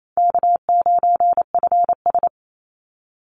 Morse code, heard as a keyed tone, about 28 wpm, 700 Hz. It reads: K9FS
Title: Easy Call Signs - 28wpm